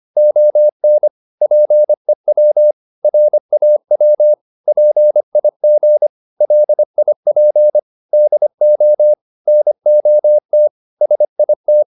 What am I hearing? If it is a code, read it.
Morse code, heard as a keyed tone, about 25 wpm, 600 Hz. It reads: ON PEW RAW PIG LIP DO NOT SIT